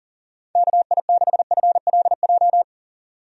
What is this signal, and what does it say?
Morse code, heard as a keyed tone, about 40 wpm, 700 Hz. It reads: KI6FLJ
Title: USA Call Signs - 6 Characters - 40wpm